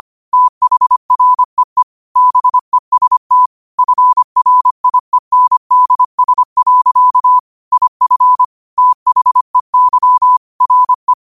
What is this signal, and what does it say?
Morse code, heard as a keyed tone, about 25 wpm, 1000 Hz. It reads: THREE BEST FRIENDS. IF THEY RE
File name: qotdfiles / 25wpm